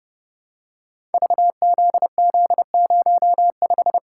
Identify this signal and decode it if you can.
Morse code, heard as a keyed tone, about 30 wpm, 700 Hz. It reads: VZZ05
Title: Code Groups - 5 Letter:Numbers - 30wpm